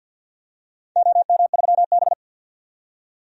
Morse code, heard as a keyed tone, about 53 wpm, 700 Hz. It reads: OM3B
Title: DXCC-100 Call Signs - 4 Characters - 50wpm